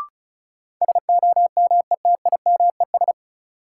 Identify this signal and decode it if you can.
Morse code, heard as a keyed tone, about 36 wpm, 700 Hz. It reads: SOMETIMES
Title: Sentences from Top 600 Words - New Word Drill - 35wpm